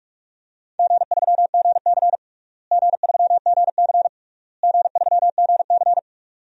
Morse code, heard as a keyed tone, about 48 wpm, 700 Hz. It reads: G3GC G3GC G3GC
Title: DXCC-100 Call Signs - 4 Characters - Repeated 3x - 45wpm